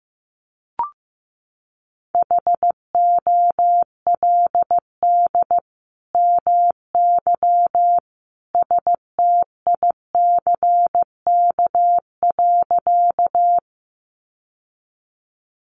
Morse code, heard as a keyed tone, about 15 wpm, 700 Hz. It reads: HOLD MY STICK.
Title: Sentences from Top 1200 Words 15wpm